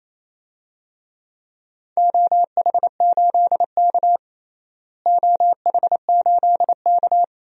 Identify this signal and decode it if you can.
Morse code, heard as a keyed tone, about 27 wpm, 700 Hz. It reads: OH8K OH8K
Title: DXCC-100 Call Signs - 4 Characters - Repeated 3x - 28wpm